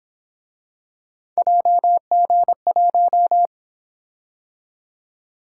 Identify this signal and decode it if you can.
Morse code, heard as a keyed tone, about 27 wpm, 700 Hz. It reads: JG1
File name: Speed-Racing - Three Letter-Number Combinations - 22wpm